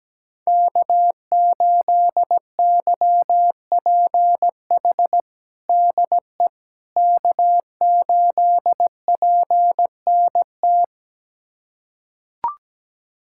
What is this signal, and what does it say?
Morse code, heard as a keyed tone, about 17 wpm, 700 Hz. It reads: K8YPH DE K8PNT
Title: USA Callsign DE Callsign - 5 Characters - 17wpm